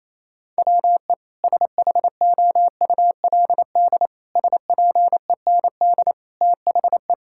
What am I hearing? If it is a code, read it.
Morse code, heard as a keyed tone, about 28 wpm, 700 Hz. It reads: WE SHOULD SPEND THE
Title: Sentences from Top 1000 Words - Review Part 5 - 28wpm